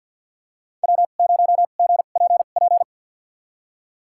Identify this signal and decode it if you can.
Morse code, heard as a keyed tone, about 51 wpm, spaced as for 36 wpm, 700 Hz. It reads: W0GPP
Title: USA Call Signs - 2x Character Spacing - 50wpm